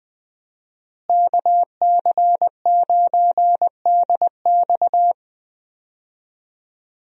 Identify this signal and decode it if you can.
Morse code, heard as a keyed tone, about 20 wpm, 700 Hz. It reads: KC9DX
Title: USA Call Signs - 5 Characters - Repeated 2x - 20wpm